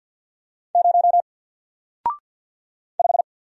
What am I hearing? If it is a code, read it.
Morse code, heard as a keyed tone, about 48 wpm, 700 Hz. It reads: 0 5
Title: Single Letter-Number Alt 50wpm